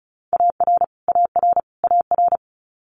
Morse code, heard as a keyed tone, about 34 wpm, 700 Hz. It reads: AR AR AR
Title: US States Repeated 3x 35wpm